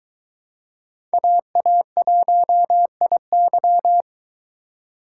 Morse code, heard as a keyed tone, about 23 wpm, 700 Hz. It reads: AA1IY